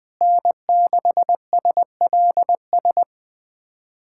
Morse code, heard as a keyed tone, about 20 wpm, 700 Hz. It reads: N6SLS
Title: USA Callsign DE Callsign - 5 Characters - 20wpm